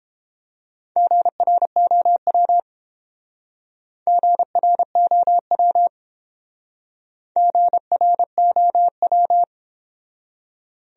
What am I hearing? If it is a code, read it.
Morse code, heard as a keyed tone, about 28 wpm, 700 Hz. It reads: GROW GROW GROW